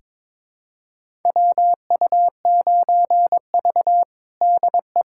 Morse code, heard as a keyed tone, about 22 wpm, 700 Hz. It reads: WU9V DE
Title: USA Callsign DE Callsign - 4 Characters - 22wpm